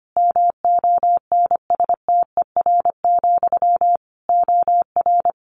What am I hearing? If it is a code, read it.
Morse code, heard as a keyed tone, about 25 wpm, 700 Hz. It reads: MONSTER, OR
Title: A Sound Of Thunder 25wpm